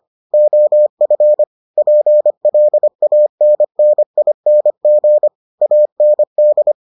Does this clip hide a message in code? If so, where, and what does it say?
Yes: Morse code from 0.3 to 6.7 s, heard as a keyed tone, about 25 wpm, 600 Hz. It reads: OF PLANNING AND